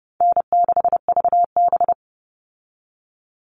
Morse code, heard as a keyed tone, about 30 wpm, 700 Hz. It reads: N6VB
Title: All Cut Numbers - 4-Digit 30wpm